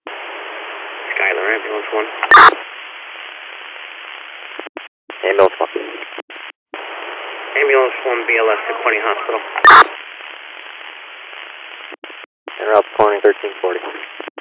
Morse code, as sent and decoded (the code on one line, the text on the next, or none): none